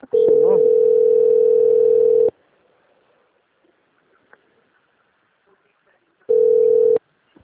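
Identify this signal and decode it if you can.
Ringback tone (North America), 440 + 480 Hz.
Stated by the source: Call progress tone on an outgoing call recorded by a call system at Atibaia - SP.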